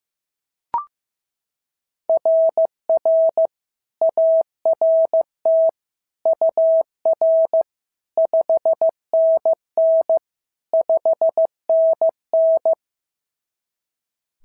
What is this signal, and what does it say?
Morse code, heard as a keyed tone, about 14 wpm, 650 Hz. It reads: RR ART UR 5NN 5NN
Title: SOTA QSO Practice 15wpm